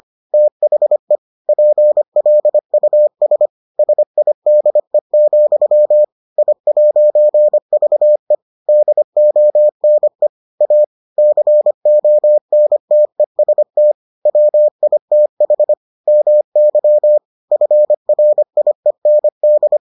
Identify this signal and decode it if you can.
Morse code, heard as a keyed tone, about 25 wpm, 600 Hz. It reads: THE PLUS SIDE, I'VE DONE A CONTEST WITH MY FRIEND